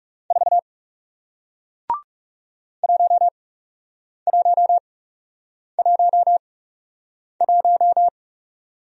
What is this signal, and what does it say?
Morse code, heard as a keyed tone, about 41 wpm, spaced as for 8 wpm, 700 Hz. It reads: 4 1111